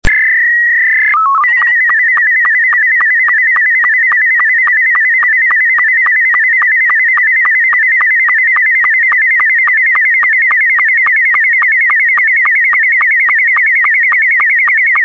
SSTV (slow-scan television), Scottie 2.
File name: sstv / sstv1 / scottie s2